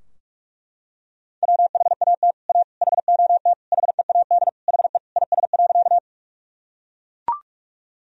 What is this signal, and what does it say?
Morse code, heard as a keyed tone, about 41 wpm, 700 Hz. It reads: WHAT A HOT HEAD HE IS.
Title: Sentences from Top 1000 Words - Review Part 3 - 45wpm